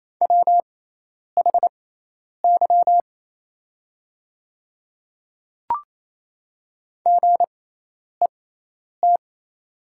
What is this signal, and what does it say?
Morse code, heard as a keyed tone, about 27 wpm, spaced as for 10 wpm, 690 Hz. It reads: WHY GET
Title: 6x Character Spacing - Top 300 Words - 28wpm